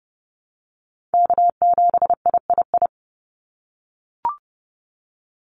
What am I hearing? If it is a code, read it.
Morse code, heard as a keyed tone, about 29 wpm, 700 Hz. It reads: K7III
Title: NA SOTA Chaser Callsigns 30wpm